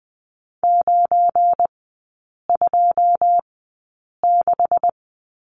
Morse code, heard as a keyed tone, about 20 wpm, spaced as for 8 wpm, 700 Hz. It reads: 926
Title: 172 Lesson 28e - Number 9 - Three Characters 20wpm